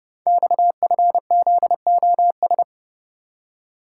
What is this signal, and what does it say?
Morse code, heard as a keyed tone, about 30 wpm, 700 Hz. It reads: XFZOS
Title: Code Groups - 5 Letters - 30wpm